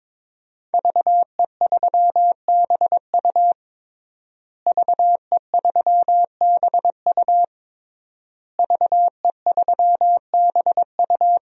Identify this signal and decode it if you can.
Morse code, heard as a keyed tone, about 22 wpm, 700 Hz. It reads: VE3BU VE3BU VE3BU
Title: DXCC-100 Call Signs - 5 Characters - Repeated 3x - 22wpm